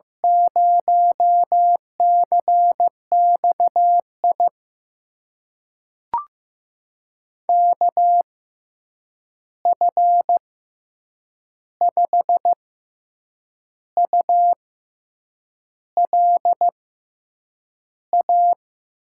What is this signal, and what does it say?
Morse code, heard as a keyed tone, about 15 wpm, spaced as for 7 wpm, 700 Hz. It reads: 0CXI KF5ULA
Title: USA Call Signs - 6 Characters - 6x Character Spacing - 15wpm